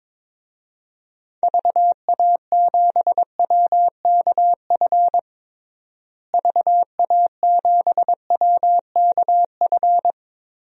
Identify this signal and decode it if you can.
Morse code, heard as a keyed tone, about 22 wpm, 700 Hz. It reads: VA7WKF VA7WKF
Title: DXCC-100 Call Signs - 6 Characters - Repeated 3x - 22wpm